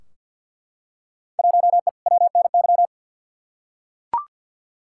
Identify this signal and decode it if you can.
Morse code, heard as a keyed tone, about 51 wpm, 700 Hz. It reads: 1E WNY
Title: Field Day Exchanges - 50wpm